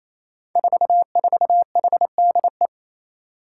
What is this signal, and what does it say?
Morse code, heard as a keyed tone, about 27 wpm, 700 Hz. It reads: 44HDE
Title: Five Challenging Letter-Number Combinations 28wpm